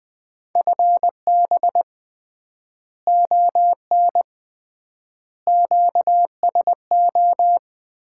Morse code, heard as a keyed tone, about 20 wpm, 700 Hz. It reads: FB ON QSO
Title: Common QSO Phrases - 3x Word-Spacing - 20wpm